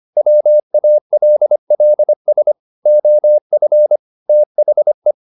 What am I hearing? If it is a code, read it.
Morse code, heard as a keyed tone, about 25 wpm, 600 Hz. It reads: WALLS OF THE
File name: the old man and the sea part 1 25wpm